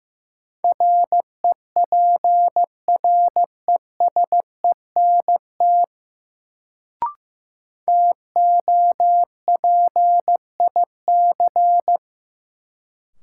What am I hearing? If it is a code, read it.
Morse code, heard as a keyed tone, about 15 wpm, 700 Hz. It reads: REPRESENT TOPIC
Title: Sentences from Top 1200 Words - New Word Drill Alt - 15wpm